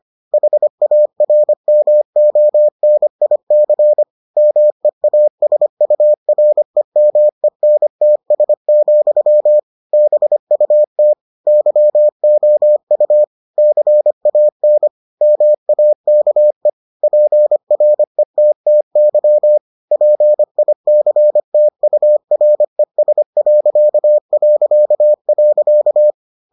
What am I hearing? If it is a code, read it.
Morse code, heard as a keyed tone, about 25 wpm, 600 Hz. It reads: HARMONIC MEASUREMENTS, BUT YOU CAN MAKE PRETTY PICTURES...